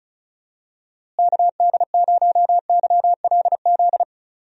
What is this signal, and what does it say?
Morse code, heard as a keyed tone, about 36 wpm, 700 Hz. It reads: KD0YLZ
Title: USA Call Signs - 6 Characters - Repeated 2x - 35wpm